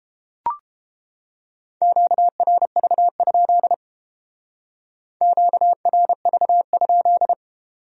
Morse code, heard as a keyed tone, about 30 wpm, 700 Hz. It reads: QRV? QRV?